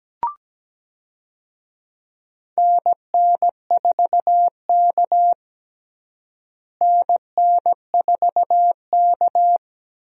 Morse code, heard as a keyed tone, about 17 wpm, 700 Hz. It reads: NN4K NN4K
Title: USA Call Signs - 4 Characters - Repeated 3x - 17wpm